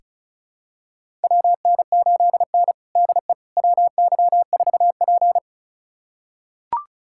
Morse code, heard as a keyed tone, about 36 wpm, 700 Hz. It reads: WN8N DE WY4P
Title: USA Callsign DE Callsign - 4 Characters - 35wpm